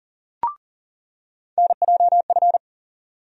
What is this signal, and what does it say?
Morse code, heard as a keyed tone, about 40 wpm, 700 Hz. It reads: NJF
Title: USA Call Sign Postfixes - 3 Letters 40wpm